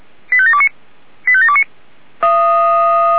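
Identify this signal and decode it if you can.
Five-tone selective call (ZVEI), heard as quick runs of five tones, tones of about 70 ms. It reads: 85729 85729
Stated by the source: ZVEI five-tone selective call received by SDR.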